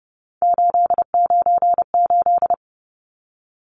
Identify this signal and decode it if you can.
Morse code, heard as a keyed tone, about 30 wpm, 700 Hz. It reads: 898
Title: Numbers - 3-Digit 30wpm